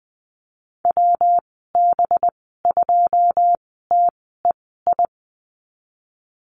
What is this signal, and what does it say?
Morse code, heard as a keyed tone, about 19 wpm, spaced as for 14 wpm, 700 Hz. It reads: WB2TEI
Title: 193 Lesson 31g - Letter v - Call Signs 20wpm